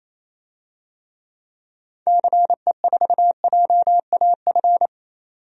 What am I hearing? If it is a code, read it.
Morse code, heard as a keyed tone, about 28 wpm, 700 Hz. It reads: CE4JAF
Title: DXCC-100 Call Signs - 6 Characters - 28wpm